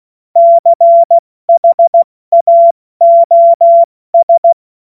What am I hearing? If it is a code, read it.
Morse code, heard as a keyed tone, about 16 wpm, 680 Hz. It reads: CHAOS
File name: Morsekurs / Witze / Witz 21.November 2023 70 F80